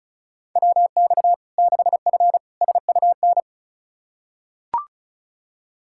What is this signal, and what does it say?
Morse code, heard as a keyed tone, about 34 wpm, 700 Hz. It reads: WX 6F SUN
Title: Practice QSO - Exchange 2 - Info - 35wpm